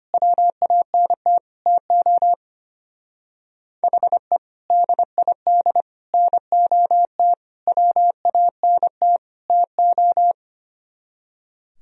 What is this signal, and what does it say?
Morse code, heard as a keyed tone, about 31 wpm, 700 Hz. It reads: WANT TO HE DID NOT WANT TO
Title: Speed-Racing - Sets of 5 Words 25wpm